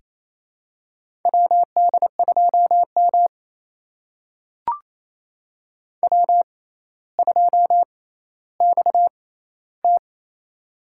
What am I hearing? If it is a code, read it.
Morse code, heard as a keyed tone, about 28 wpm, spaced as for 14 wpm, 700 Hz. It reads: WD2M W2XT
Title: USA Call Signs - 4 Characters - 6x Character Spacing - 28wpm